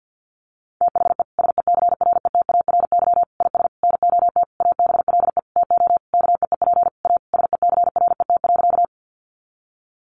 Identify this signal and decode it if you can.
Morse code, heard as a keyed tone, about 49 wpm, 700 Hz. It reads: THE SECRETARY IS NOT ABLE TO KEEP A SECRET.
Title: Sentences from Top 1300 Words 50wpm